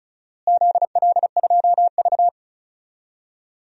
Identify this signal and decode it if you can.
Morse code, heard as a keyed tone, about 36 wpm, 700 Hz. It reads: ZL2V
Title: DXCC-100 Call Signs - 4 Characters - Repeated 2x - 35wpm